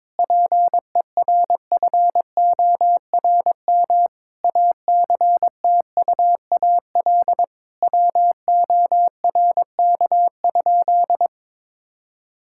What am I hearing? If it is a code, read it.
Morse code, heard as a keyed tone, about 22 wpm, 700 Hz. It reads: PERFORM ACTUAL WORK?
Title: Sentences from Top 1000 Words - Review Part 5 - 22wpm